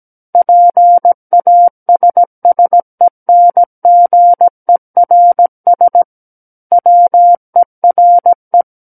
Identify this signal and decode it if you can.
Morse code, heard as a keyed tone, about 17 wpm, 700 Hz. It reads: PASSENGERS WERE
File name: Book One / 08-Chapter 8 - 17 WPM